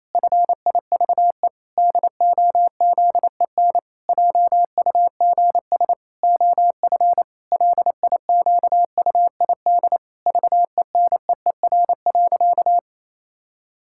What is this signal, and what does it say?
Morse code, heard as a keyed tone, about 28 wpm, 700 Hz. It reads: FIVE DOZEN JUGS OF LIQUID VENEER.
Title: Pangram Challenge v2 - 28wpm